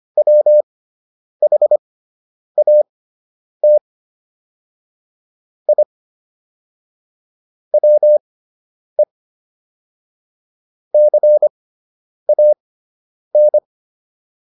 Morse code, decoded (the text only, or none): WHAT I WE CAN